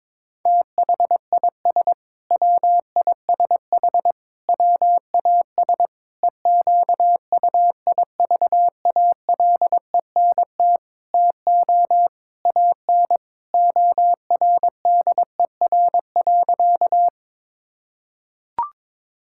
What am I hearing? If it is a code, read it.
Morse code, heard as a keyed tone, about 22 wpm, 700 Hz. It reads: THIS WISH WAS EQUIVALENT TO AN ORDER.
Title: Sentences from Top 1600 Words 22wpm